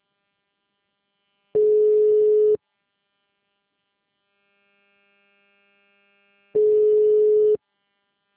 Ringback tone (Europe, 425 Hz).